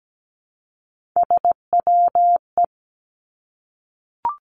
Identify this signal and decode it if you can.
Morse code, heard as a keyed tone, about 17 wpm, 700 Hz. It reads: SWE
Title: Three Challenging Letter-Number Combinations 17wpm